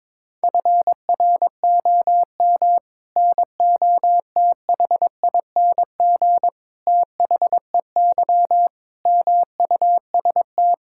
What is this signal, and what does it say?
Morse code, heard as a keyed tone, about 22 wpm, 700 Hz. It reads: FROM NOTHING THEY MUST
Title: Sentences from Top 1000 Words - Review Part 5 - 22wpm